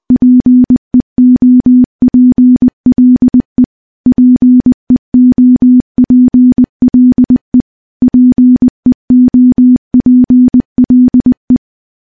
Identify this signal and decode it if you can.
Morse code, heard as a keyed tone, about 20 wpm, 250 Hz. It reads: PEOPLE PEOPLE PEOPLE
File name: morse-training-20WPM